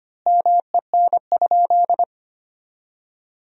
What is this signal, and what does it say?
Morse code, heard as a keyed tone, about 25 wpm, 700 Hz. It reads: MEN?
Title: 2x Word-Spacing - Sentences from Top 200 Words - 25wpm